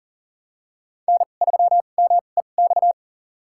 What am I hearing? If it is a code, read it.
Morse code, heard as a keyed tone, about 40 wpm, spaced as for 29 wpm, 700 Hz. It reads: N3MEX
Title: USA Call Signs - 5 Characters - 2x Character Spacing - 40wpm